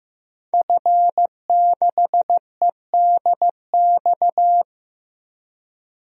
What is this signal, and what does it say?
Morse code, heard as a keyed tone, about 15 wpm, 700 Hz. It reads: F6EDX
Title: DXCC-100 Call Signs - 5 Characters - Repeated 3x - 15wpm